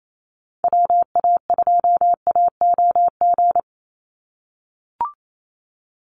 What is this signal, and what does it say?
Morse code, heard as a keyed tone, about 28 wpm, 700 Hz. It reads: WA2AOG
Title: Call Signs - 28wpm